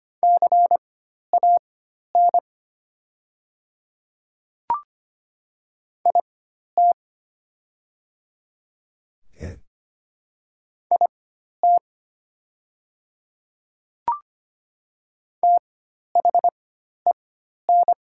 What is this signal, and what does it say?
Morse code, heard as a keyed tone, about 25 wpm, spaced as for 12 wpm, 700 Hz. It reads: CAN IT IT THEN
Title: 4x Character Spacing - Top 100 Words - 25wpm